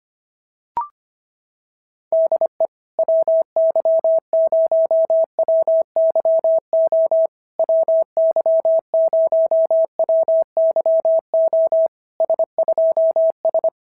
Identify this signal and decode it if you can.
Morse code, heard as a keyed tone, about 25 wpm, 650 Hz. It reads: DE WY0WYO WY0WYO S2S
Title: SOTA QSO Practice 25wpm